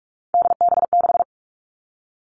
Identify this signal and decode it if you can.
Morse code, heard as a keyed tone, about 46 wpm, 700 Hz. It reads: DB6
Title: Three Challenging Letter-Number Combinations 45wpm